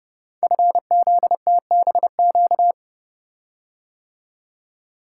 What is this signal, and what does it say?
Morse code, heard as a keyed tone, about 30 wpm, 700 Hz. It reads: FZTBQ
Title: Code Groups - 5 Letters - 30wpm